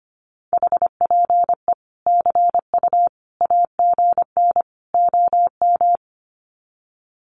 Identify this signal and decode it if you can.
Morse code, heard as a keyed tone, about 25 wpm, 700 Hz. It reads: HPE CU AGN OM
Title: Common QSO Phrases 25wpm